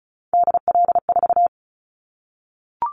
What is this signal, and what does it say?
Morse code, heard as a keyed tone, about 36 wpm, 700 Hz. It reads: DL4
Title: Callsign Prefixes for Top 25 DXCC - 35wpm